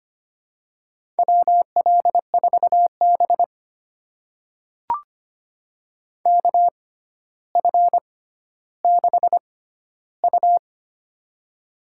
Morse code, heard as a keyed tone, about 25 wpm, spaced as for 13 wpm, 700 Hz. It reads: WL4B KF6U
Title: USA Call Signs - 6x Character Spacing - 25wpm